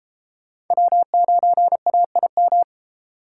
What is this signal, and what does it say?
Morse code, heard as a keyed tone, about 31 wpm, 700 Hz. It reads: W9AIM